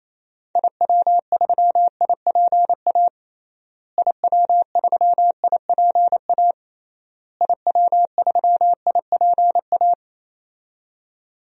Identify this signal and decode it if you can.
Morse code, heard as a keyed tone, about 28 wpm, 700 Hz. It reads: IW3IPA IW3IPA IW3IPA
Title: DXCC-100 Call Signs - 6 Characters - Repeated 3x - 28wpm